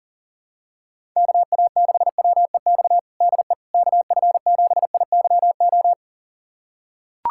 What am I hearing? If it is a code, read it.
Morse code, heard as a keyed tone, about 40 wpm, 700 Hz. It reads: KA6WEX DE KF7IYO
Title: USA Callsign DE Callsign - 6 Characters - 40wpm